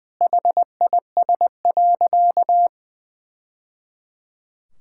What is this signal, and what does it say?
Morse code, heard as a keyed tone, about 20 wpm, 700 Hz. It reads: HIS.
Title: Sentences from Top 1000 Words - Review Part 6 - 20wpm